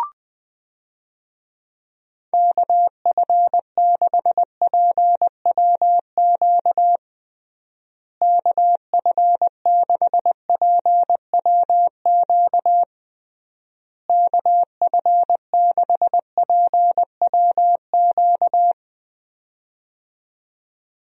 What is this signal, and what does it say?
Morse code, heard as a keyed tone, about 20 wpm, 700 Hz. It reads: KF6PWQ KF6PWQ KF6PWQ
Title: USA Call Signs - 6 Characters - Repeated 3x - 20wpm